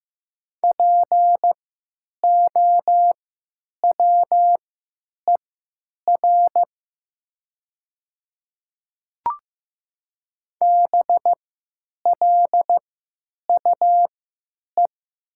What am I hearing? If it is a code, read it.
Morse code, heard as a keyed tone, about 15 wpm, spaced as for 9 wpm, 700 Hz. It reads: POWER BLUE
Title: 3x Character Spacing - Top 500 Words - 15wpm